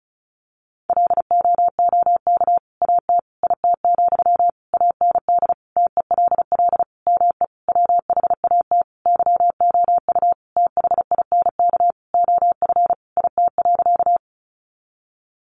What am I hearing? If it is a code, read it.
Morse code, heard as a keyed tone, about 34 wpm, 700 Hz. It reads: LOOK AT IT, AND TELL ME WHAT YOU THINK OF IT.
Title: Sentences from Top 200 Words 35wpm